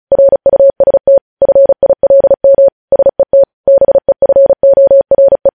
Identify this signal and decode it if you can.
Morse code, heard as a keyed tone, about 36 wpm, 550 Hz. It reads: RUST FILM SET BEFORE
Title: News Headlines at 35